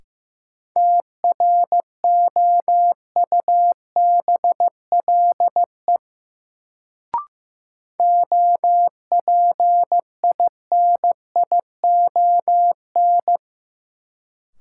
Morse code, heard as a keyed tone, about 15 wpm, 700 Hz. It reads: TROUBLE OPINION
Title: Sentences from Top 1200 Words - New Word Drill Alt - 15wpm